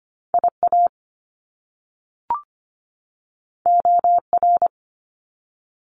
Morse code, heard as a keyed tone, about 25 wpm, 700 Hz. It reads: IA OR
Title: US States 25wpm